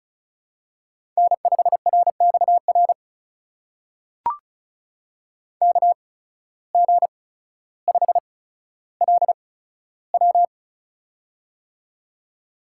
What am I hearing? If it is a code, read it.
Morse code, heard as a keyed tone, about 36 wpm, spaced as for 15 wpm, 700 Hz. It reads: N5RXR KG5LW
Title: USA Call Signs - 5 Characters - 8x Character Spacing - 35wpm